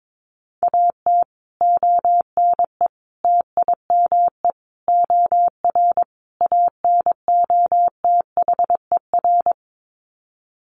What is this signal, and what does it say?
Morse code, heard as a keyed tone, about 22 wpm, 700 Hz. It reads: AT ONE TIME OR ANOTHER
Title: Sets of 5 Words Encore 22wpm